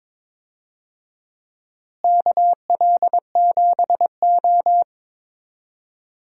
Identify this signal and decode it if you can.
Morse code, heard as a keyed tone, about 22 wpm, 700 Hz. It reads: KL7O
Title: DXCC-100 Call Signs - 4 Characters - Repeated 3x - 22wpm